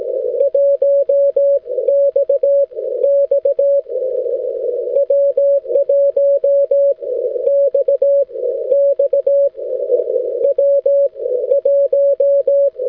Morse code, heard as a keyed tone, about 18 wpm, spaced as for 13 wpm, 560 Hz. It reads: 1 XX W1 XX W1
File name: wl7e-echo-160m